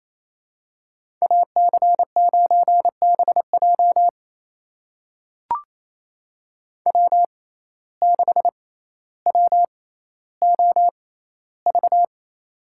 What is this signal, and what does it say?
Morse code, heard as a keyed tone, about 28 wpm, spaced as for 14 wpm, 700 Hz. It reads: AC9BJ W6WOV
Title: USA Call Signs - 5 Characters - 6x Character Spacing - 28wpm